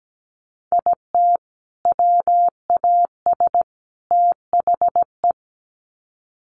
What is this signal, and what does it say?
Morse code, heard as a keyed tone, about 17 wpm, 700 Hz. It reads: IT WAS THE
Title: Sets of 3 Words 17wpm